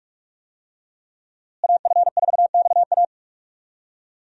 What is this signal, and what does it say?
Morse code, heard as a keyed tone, about 47 wpm, 700 Hz. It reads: AF4XA
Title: USA Call Signs - 5 Characters - Repeated 2x - 45wpm